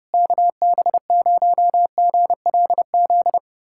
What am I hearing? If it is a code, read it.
Morse code, heard as a keyed tone, about 30 wpm, 700 Hz. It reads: KB0GLZ
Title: USA Callsign DE Callsign - 6 Characters - 30wpm